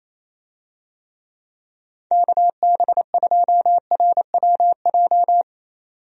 Morse code, heard as a keyed tone, about 27 wpm, 700 Hz. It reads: KB2RWJ
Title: USA Call Signs - 6 Characters - 28wpm